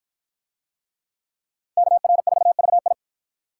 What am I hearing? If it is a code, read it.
Morse code, heard as a keyed tone, about 53 wpm, 700 Hz. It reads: KR4VI